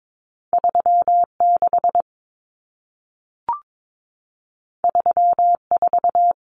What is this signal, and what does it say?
Morse code, heard as a keyed tone, about 22 wpm, 700 Hz. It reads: 36 34
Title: Numbers - 2-Digit 22wpm